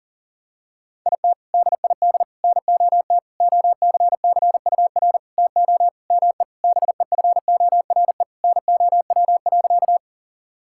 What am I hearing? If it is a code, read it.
Morse code, heard as a keyed tone, about 40 wpm, 700 Hz. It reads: IT DID NOT OCCUR TO ME BEFORE NOW.
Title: Sentences from Top 1000 Words - Review Part 6 - 40wpm